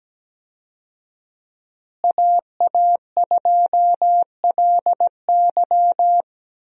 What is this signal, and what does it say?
Morse code, heard as a keyed tone, about 17 wpm, 700 Hz. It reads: AA2LY
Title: USA Call Signs - 5 Characters - Repeated 3x - 17wpm